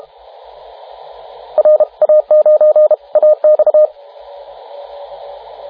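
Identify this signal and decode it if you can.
Morse code, heard as a keyed tone, about 32 wpm, 610 Hz, with radio noise. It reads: RA9AX
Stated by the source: CW contest exchange in the 2014 Popov Cup VHF contest, RA9AX calling RA9BZ, recorded by the TR4W logger.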